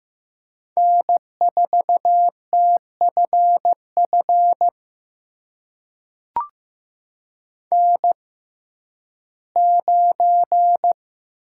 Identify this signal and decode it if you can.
Morse code, heard as a keyed tone, about 15 wpm, spaced as for 11 wpm, 700 Hz. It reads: N4TFF N9
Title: USA Call Signs - 5 Characters - 6x Character Spacing - 15wpm